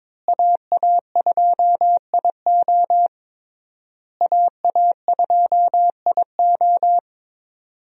Morse code, heard as a keyed tone, about 22 wpm, 700 Hz. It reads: AA2IO AA2IO
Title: USA Call Signs - 5 Characters - Repeated 3x - 22wpm